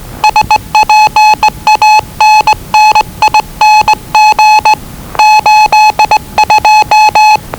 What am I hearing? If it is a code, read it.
Morse code, heard as a keyed tone, about 20 wpm, 860 Hz. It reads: SPANNING 82
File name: WellingtonRadioZlwCloseDownOnHfAndMf / 01-Zlw 4277khz 417.5 qrt until 0925z